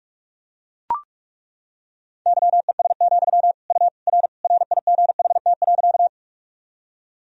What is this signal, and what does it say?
Morse code, heard as a keyed tone, about 41 wpm, 690 Hz. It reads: YES, U R RIGHT.
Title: Sentences from Top 500 Words and Abbreviations 45wpm